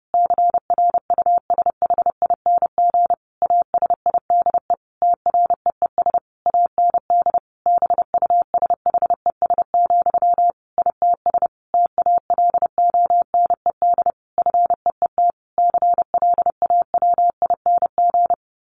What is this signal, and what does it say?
Morse code, heard as a keyed tone, about 30 wpm, 700 Hz. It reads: CRUSHING ASIDE TREES AND BUSHES, ITS TALONED FEET CLAWING
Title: A Sound Of Thunder 30wpm